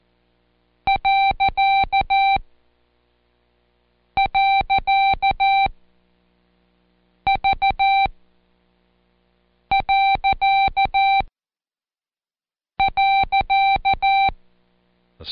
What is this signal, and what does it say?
Morse code, heard as a keyed tone, about 14 wpm, spaced as for 5 wpm, 780 Hz. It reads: ..V..
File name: Code Quick / AudioDownld1 / Seg63-Day7-Pract PeriodMV